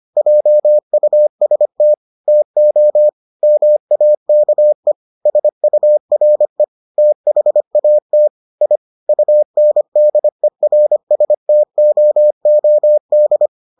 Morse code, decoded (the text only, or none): JUST TO MAKE SURE THAT I UNDERSTOOD